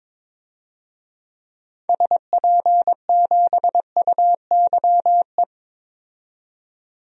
Morse code, heard as a keyed tone, about 22 wpm, 700 Hz. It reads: SP7UYE
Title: DXCC-100 Call Signs - 6 Characters - 22wpm